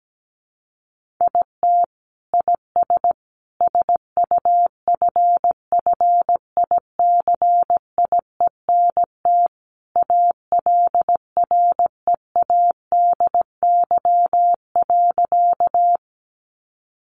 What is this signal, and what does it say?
Morse code, heard as a keyed tone, about 17 wpm, 700 Hz. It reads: IT IS SUFFICIENT ALREADY.
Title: Sentences from Top 1400 Words 17wpm